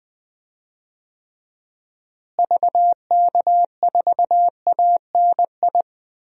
Morse code, heard as a keyed tone, about 20 wpm, 700 Hz. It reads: VK4ANI
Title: DXCC-100 Call Signs - 6 Characters - Repeated 2x - 20wpm